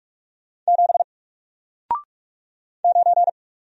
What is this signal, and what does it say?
Morse code, heard as a keyed tone, about 46 wpm, 700 Hz. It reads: 7 9
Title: Single Letter-Number Alt 45wpm